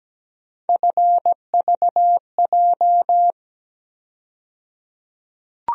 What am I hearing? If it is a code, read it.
Morse code, heard as a keyed tone, about 17 wpm, 700 Hz. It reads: FVJ
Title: USA Call Sign Postfixes - 3 Letters 17wpm